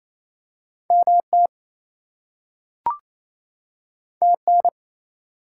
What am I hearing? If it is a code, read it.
Morse code, heard as a keyed tone, about 28 wpm, 700 Hz. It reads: MT TN
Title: US States and Canadian Provinces 28wpm